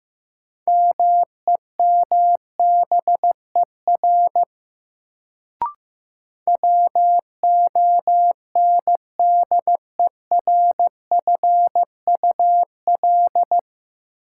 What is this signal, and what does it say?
Morse code, heard as a keyed tone, about 15 wpm, 700 Hz. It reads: MEMBER WONDERFUL
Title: Sentences from Top 1100 Words - New Word Drill Alt - 15wpm